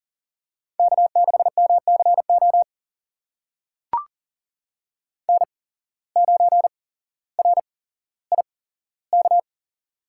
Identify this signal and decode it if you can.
Morse code, heard as a keyed tone, about 40 wpm, spaced as for 17 wpm, 700 Hz. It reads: K6MCO N9RIK
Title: USA Call Signs - 5 Characters - 8x Character Spacing - 40wpm